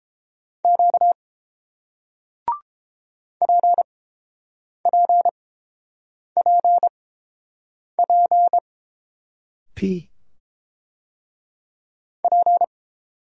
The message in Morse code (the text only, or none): Q PPPP P